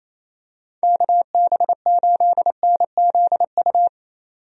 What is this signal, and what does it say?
Morse code, heard as a keyed tone, about 28 wpm, 700 Hz. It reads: KB8NZU
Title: USA Call Signs - 6 Characters - 28wpm